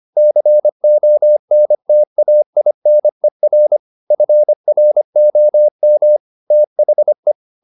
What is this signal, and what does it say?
Morse code, heard as a keyed tone, about 25 wpm, 600 Hz. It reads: CONTAINER FROM THE
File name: the old man and the sea part 1 25wpm